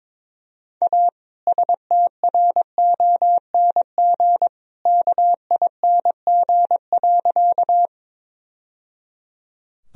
Morse code, heard as a keyed tone, about 22 wpm, 700 Hz. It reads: A STRONG KING.
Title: Sentences from Top 1000 Words - Review Part 3 - 22wpm